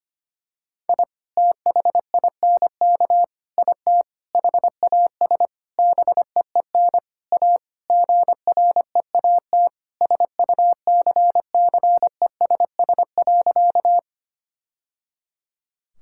Morse code, heard as a keyed tone, about 25 wpm, 700 Hz. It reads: I THINK IT HAS BEEN A GREAT SUCCESS.
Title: Sentences from Top 1000 Words - Review Part 6 - 25wpm